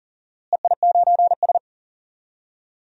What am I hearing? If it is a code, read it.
Morse code, heard as a keyed tone, about 40 wpm, 700 Hz. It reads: EI9S
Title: DXCC-100 Call Signs - 4 Characters - Repeated 3x - 40wpm